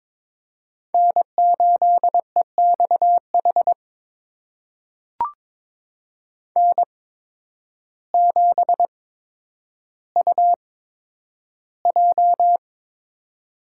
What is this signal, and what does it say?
Morse code, heard as a keyed tone, about 22 wpm, spaced as for 10 wpm, 700 Hz. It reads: N8EXH N7UJ
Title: USA Call Signs - 5 Characters - 8x Character Spacing - 22wpm